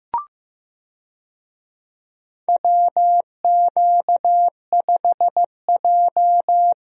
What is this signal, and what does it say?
Morse code, heard as a keyed tone, about 15 wpm, 700 Hz. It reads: WQ5J
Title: USA Call Signs - 4 Characters - Repeated 2x - 15wpm